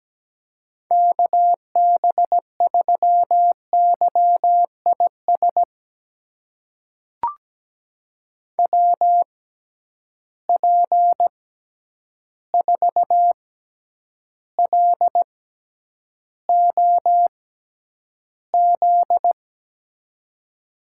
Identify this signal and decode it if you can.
Morse code, heard as a keyed tone, about 17 wpm, spaced as for 9 wpm, 700 Hz. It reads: KB3YIS WP4LOZ
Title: USA Call Signs - 6 Characters - 6x Character Spacing - 17wpm